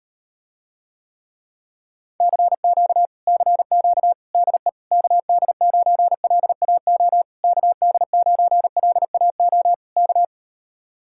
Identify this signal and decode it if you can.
Morse code, heard as a keyed tone, about 39 wpm, 700 Hz. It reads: CQ CQ DE KD9LAO KD9LAO K